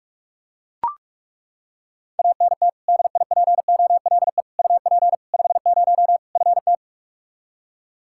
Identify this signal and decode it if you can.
Morse code, heard as a keyed tone, about 41 wpm, 700 Hz. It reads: ANT DIPOLE UP 50 FT
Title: K1USN SST QSO Practice 45wpm